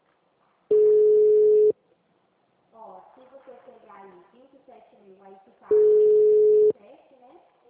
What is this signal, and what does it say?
Ringback tone (Europe, 425 Hz), with speech.